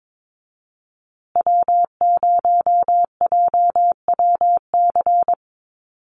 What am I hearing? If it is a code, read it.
Morse code, heard as a keyed tone, about 22 wpm, 700 Hz. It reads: W0JWC
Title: USA Call Signs Repeated 2x 22wpm